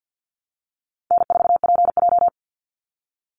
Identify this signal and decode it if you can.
Morse code, heard as a keyed tone, about 50 wpm, 700 Hz. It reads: N4PJ
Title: Easy Call Signs Repeated 2x - 50wpm